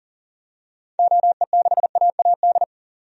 Morse code, heard as a keyed tone, about 40 wpm, 700 Hz. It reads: OE6AAD
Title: DXCC-100 Call Signs - 6 Characters - 40wpm